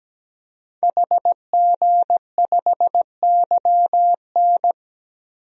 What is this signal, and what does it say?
Morse code, heard as a keyed tone, about 17 wpm, 700 Hz. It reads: HG5YN
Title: DXCC-100 Call Signs - 5 Characters - 17wpm